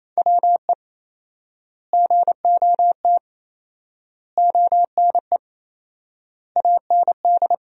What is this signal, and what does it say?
Morse code, heard as a keyed tone, about 27 wpm, 700 Hz. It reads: WE GOT ONE AND
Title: 4x Word-Spacing - Sentences from Top 300 Words - 28wpm